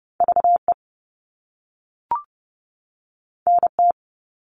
Morse code, heard as a keyed tone, about 30 wpm, 700 Hz. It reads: VE NT
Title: All Cut Numbers - 2-Digit 30wpm